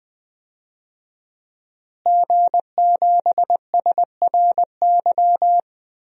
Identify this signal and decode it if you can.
Morse code, heard as a keyed tone, about 20 wpm, 700 Hz. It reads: G7SRY
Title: DXCC-100 Call Signs - 5 Characters - Repeated 2x - 20wpm